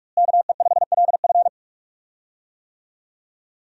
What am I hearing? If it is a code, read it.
Morse code, heard as a keyed tone, about 47 wpm, 700 Hz. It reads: KE5LF
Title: USA Call Signs - 5 Characters - Repeated 2x - 45wpm